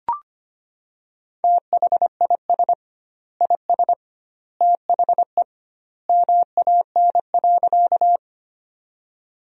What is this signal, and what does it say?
Morse code, heard as a keyed tone, about 25 wpm, 700 Hz. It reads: THIS IS THE MAN.
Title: 2x Word-Spacing - Sentences from Top 200 Words - 25wpm